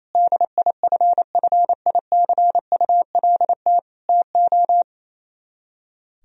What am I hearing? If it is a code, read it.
Morse code, heard as a keyed tone, about 32 wpm, 700 Hz. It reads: DIFFICULT TO
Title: Speed-Racing - Sets of 4 Words 28wpm